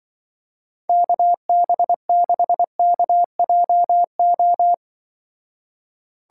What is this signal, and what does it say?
Morse code, heard as a keyed tone, about 23 wpm, 700 Hz. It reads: KB6KJO
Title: Speed Racing - USA 2x3 Call Signs 20wpm